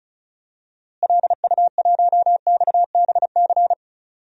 Morse code, heard as a keyed tone, about 34 wpm, 700 Hz. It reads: LU1XBC
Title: DXCC-100 Call Signs - 6 Characters - Repeated 3x - 35wpm